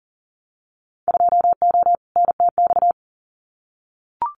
Morse code, heard as a keyed tone, about 40 wpm, 700 Hz. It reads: 2O NTX
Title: Winter Field Day Exchanges Encore 40wpm